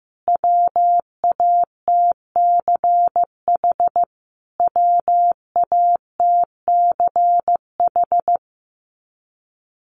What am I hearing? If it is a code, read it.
Morse code, heard as a keyed tone, about 15 wpm, 700 Hz. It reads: WATCH WATCH
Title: Top 300 Words Repeated 15wpm